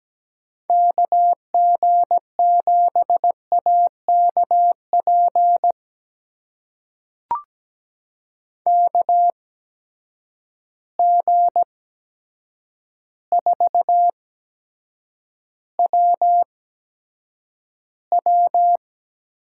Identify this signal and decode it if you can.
Morse code, heard as a keyed tone, about 17 wpm, spaced as for 8 wpm, 700 Hz. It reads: KG7AKP KG4WW
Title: USA Call Signs - 6 Characters - 8x Character Spacing - 17wpm